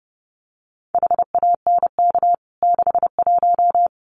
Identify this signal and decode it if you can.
Morse code, heard as a keyed tone, about 30 wpm, 700 Hz. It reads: HANK 61
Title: CWT Exchange 30wpm